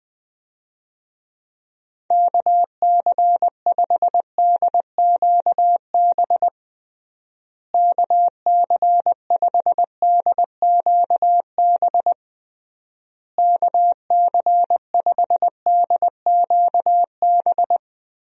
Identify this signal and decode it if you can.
Morse code, heard as a keyed tone, about 20 wpm, 700 Hz. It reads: KC5DQB KC5DQB KC5DQB
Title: USA Call Signs - 6 Characters - Repeated 3x - 20wpm